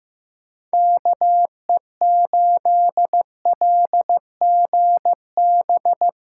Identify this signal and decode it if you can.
Morse code, heard as a keyed tone, about 15 wpm, 700 Hz. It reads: KE8LGB
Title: USA Callsign DE Callsign - 6 Characters - 15wpm